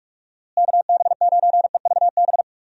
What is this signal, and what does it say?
Morse code, heard as a keyed tone, about 46 wpm, 700 Hz. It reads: KB9EVB
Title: USA Call Signs - 6 Characters - Repeated 3x - 45wpm